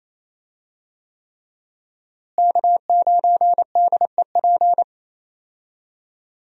Morse code, heard as a keyed tone, about 28 wpm, 700 Hz. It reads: K9DEP
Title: USA Call Signs - 5 Characters - Repeated 3x - 28wpm